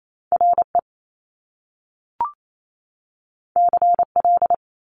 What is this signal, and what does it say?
Morse code, heard as a keyed tone, about 28 wpm, 700 Hz. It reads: RE CL
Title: Top Two Letter Combinations in Words 28wpm